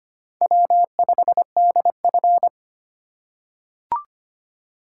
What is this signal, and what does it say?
Morse code, heard as a keyed tone, about 25 wpm, 700 Hz. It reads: W5DF
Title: USA Call Signs - 4 Characters - 2x Character Spacing - 25wpm